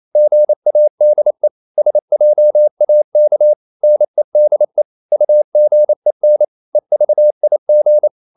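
Morse code, heard as a keyed tone, about 28 wpm, 600 Hz. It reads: GADE SJAK NEDE UGEN EVIG
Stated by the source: Danish short words sent in Morse at 28 wpm, from CW med Gnister.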